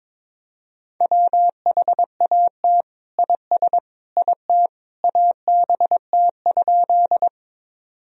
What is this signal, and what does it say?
Morse code, heard as a keyed tone, about 22 wpm, 700 Hz. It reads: WHAT IS IT ABT?
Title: Sentences from Top 100 Words and Abbreviations 22wpm